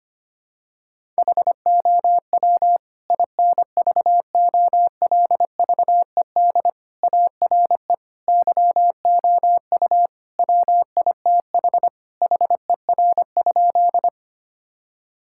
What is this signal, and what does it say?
Morse code, heard as a keyed tone, about 25 wpm, 700 Hz. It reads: HOW INVOLVED ARE YOU WITH HER?
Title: Sentences from Top 1000 Words - Review Part 5 - 25wpm